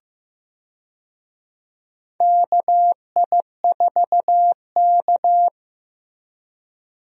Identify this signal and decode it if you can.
Morse code, heard as a keyed tone, about 15 wpm, 700 Hz. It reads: KI4K
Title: USA Call Signs - 4 Characters - Repeated 3x - 15wpm